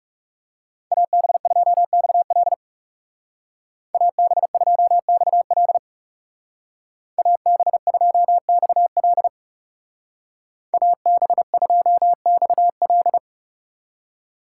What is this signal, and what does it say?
Morse code, heard as a keyed tone, about 37 wpm, 700 Hz. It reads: AB2XL AB2XL AB2XL AB2XL
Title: Speed Racing - USA 2x2 Call Signs 30wpm